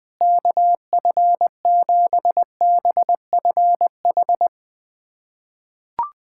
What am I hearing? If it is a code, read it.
Morse code, heard as a keyed tone, about 20 wpm, 700 Hz. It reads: KF7BFH
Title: USA Callsign DE Callsign 20wpm